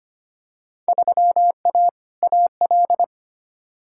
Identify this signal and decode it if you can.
Morse code, heard as a keyed tone, about 25 wpm, 700 Hz. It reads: 3A AL
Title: Field Day Exchanges - 25wpm